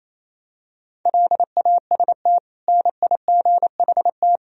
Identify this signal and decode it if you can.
Morse code, heard as a keyed tone, about 28 wpm, 700 Hz. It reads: LAST NIGHT
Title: Sentences from Top 1000 Words - Review Part 4 - 28wpm